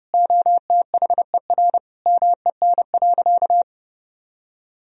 Morse code, heard as a keyed tone, about 30 wpm, 700 Hz. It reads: OTHER MEN.
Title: Sentences from Top 1000 Words - Review Part 3 - 30wpm